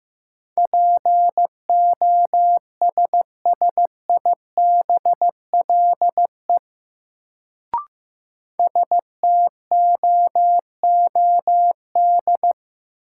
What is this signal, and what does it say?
Morse code, heard as a keyed tone, about 15 wpm, 700 Hz. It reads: POSSIBLE STOOD
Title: Sentences from Top 500 Words - New Word Drill - 15wpm